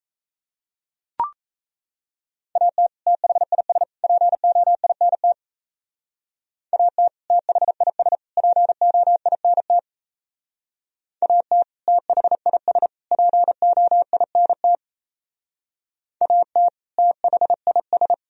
Morse code, heard as a keyed tone, about 35 wpm, 700 Hz. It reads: AT THIS POINT AT THIS POINT AT THIS POINT AT THIS